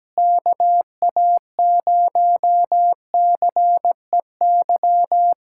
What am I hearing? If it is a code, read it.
Morse code, heard as a keyed tone, about 17 wpm, 700 Hz. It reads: KA0CEY
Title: USA Callsign DE Callsign 17wpm